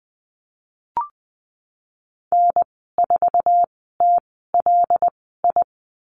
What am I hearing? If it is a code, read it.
Morse code, heard as a keyed tone, about 20 wpm, spaced as for 15 wpm, 700 Hz. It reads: N4TLI
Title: 088 Lesson 15g - Number 5 - Call Signs 20wpm